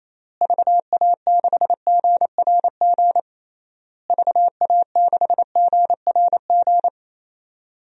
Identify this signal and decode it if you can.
Morse code, heard as a keyed tone, about 28 wpm, 700 Hz. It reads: VA6GRG VA6GRG
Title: DXCC-100 Call Signs - 6 Characters - Repeated 3x - 28wpm